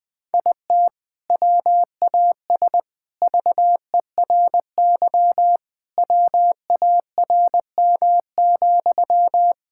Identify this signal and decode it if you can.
Morse code, heard as a keyed tone, about 21 wpm, 700 Hz. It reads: IT WAS VERY WARM,
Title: Sentences from Top 1000 Words - Review Part 5 - 20wpm